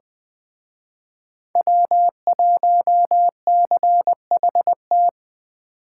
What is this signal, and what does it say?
Morse code, heard as a keyed tone, about 20 wpm, 700 Hz. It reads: W1CHT
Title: Speed Racing - USA Call Signs 17wpm